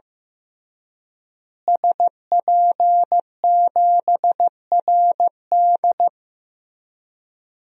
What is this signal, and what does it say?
Morse code, heard as a keyed tone, about 15 wpm, 700 Hz. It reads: SP7RD
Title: DXCC-100 Call Signs - 5 Characters - Repeated 2x - 15wpm